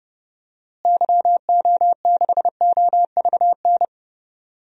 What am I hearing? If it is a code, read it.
Morse code, heard as a keyed tone, about 31 wpm, 700 Hz. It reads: YO6OVN
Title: DXCC-100 Call Signs - 6 Characters - Repeated 2x - 30wpm